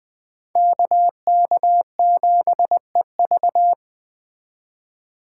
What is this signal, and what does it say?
Morse code, heard as a keyed tone, about 20 wpm, 700 Hz. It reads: KK7EV